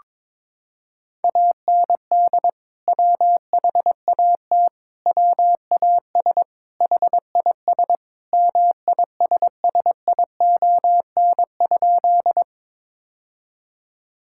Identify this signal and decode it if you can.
Morse code, heard as a keyed tone, about 22 wpm, 700 Hz. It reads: AND WHAT WAS HIS MISSION?
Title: Sentences from Top 1600 Words 22wpm